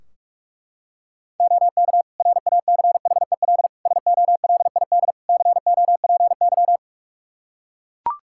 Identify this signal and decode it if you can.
Morse code, heard as a keyed tone, about 41 wpm, 700 Hz. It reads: OK RACHEL SOLID COPY
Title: K1USN SST QSO Practice 45wpm